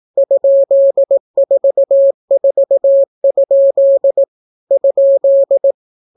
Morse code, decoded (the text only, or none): ?44? ?